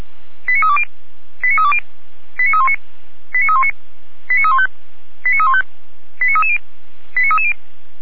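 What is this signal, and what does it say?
Five-tone selective call (ZVEI), heard as quick runs of five tones, tones of about 70 ms. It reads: 89320 89320 89319 89319 89315 89315 89330 89330
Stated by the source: ZVEI five-tone call 89110 received by SDR on channel 455.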